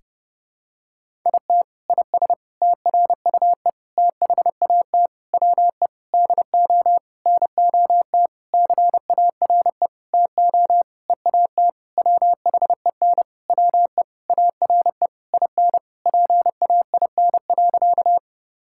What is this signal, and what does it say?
Morse code, heard as a keyed tone, about 30 wpm, 700 Hz. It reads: IT IS TRUE THAT WE DO NOT CARE TO EAT WHEN WE ARE IN PAIN.
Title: Sentences from Top 1600 Words 30wpm